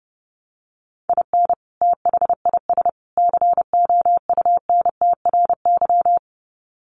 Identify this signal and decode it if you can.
Morse code, heard as a keyed tone, about 29 wpm, 700 Hz. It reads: IN THIS COUNTRY
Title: Sets of 3 Words 30wpm